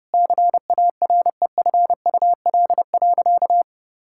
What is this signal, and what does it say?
Morse code, heard as a keyed tone, about 32 wpm, 700 Hz. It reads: CAREFUL.
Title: Sentences from Top 1600 Words 30wpm